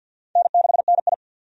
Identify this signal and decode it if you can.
Morse code, heard as a keyed tone, about 51 wpm, 700 Hz. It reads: N6NI
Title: USA Call Signs - 4 Characters - 50wpm